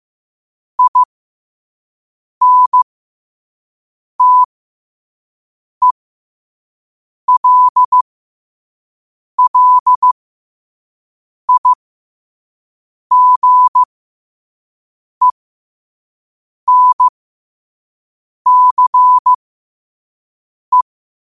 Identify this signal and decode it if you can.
Morse code, heard as a keyed tone, about 15 wpm, spaced as for 5 wpm, 1000 Hz. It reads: INTELLIGENCE
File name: 5wpm